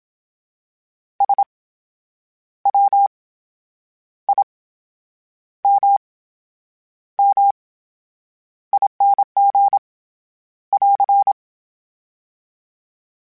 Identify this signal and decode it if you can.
Morse code, heard as a keyed tone, about 27 wpm, spaced as for 8 wpm, 800 Hz. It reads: SWIMMING <AR>
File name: Special INGs / Special INGs-mp3s / VerySlowToFaster Ing @25